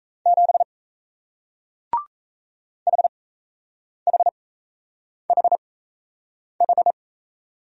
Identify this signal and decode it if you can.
Morse code, heard as a keyed tone, about 38 wpm, spaced as for 8 wpm, 700 Hz. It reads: 7 HHHH